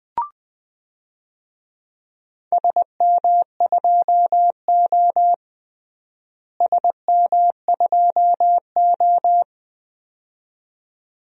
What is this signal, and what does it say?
Morse code, heard as a keyed tone, about 20 wpm, 700 Hz. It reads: SM2O SM2O
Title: DXCC-100 Call Signs - 4 Characters - Repeated 2x - 20wpm